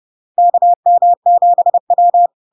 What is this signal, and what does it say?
Morse code, heard as a keyed tone, about 30 wpm, 700 Hz. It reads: KM7W
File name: LARK-FD-2022-Log-in-CW-30wpm-700-Hz